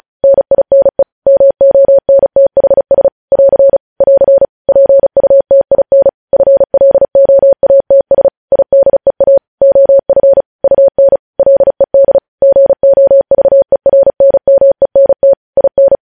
Morse code, heard as a keyed tone, about 36 wpm, 550 Hz. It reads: NINE MONTHS <AR> <AR> PUTIN FLOATS IDEA OF UN LED GOVERNMENT IN
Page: News headlines in Morse code at 35 wpm